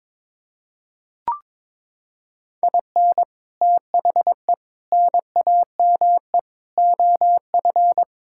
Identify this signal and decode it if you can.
Morse code, heard as a keyed tone, about 22 wpm, 700 Hz. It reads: IN THE NAME OF
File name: practice / 22wpm / Sentences with 4-Word Phrases - 22wpm